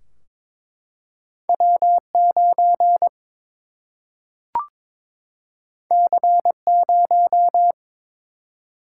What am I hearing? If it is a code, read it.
Morse code, heard as a keyed tone, about 22 wpm, 700 Hz. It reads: W 9 C 0
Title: Single-Letter Single-Number 22wpm